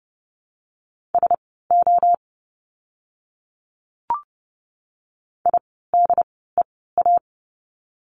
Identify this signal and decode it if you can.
Morse code, heard as a keyed tone, about 30 wpm, spaced as for 17 wpm, 700 Hz. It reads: SO IDEA
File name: Lesson 17f - Letter u - Words 30wpm